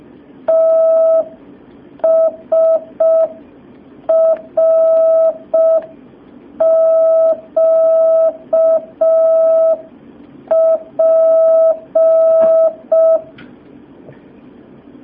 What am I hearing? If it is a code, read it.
Morse code, heard as a keyed tone, about 5 wpm, 650 Hz. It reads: TSRQP
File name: jh8cbh / 2019cw